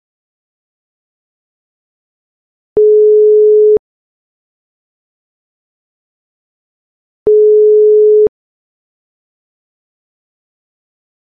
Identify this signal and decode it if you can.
Ringback tone (France), 440 Hz.
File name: Phone-ringback-tone-sound-effects